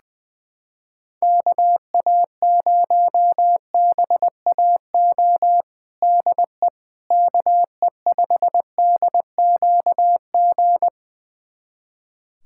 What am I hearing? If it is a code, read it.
Morse code, heard as a keyed tone, about 20 wpm, 700 Hz. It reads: KA0BAO DE KE5DQG
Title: USA Callsign DE Callsign 20wpm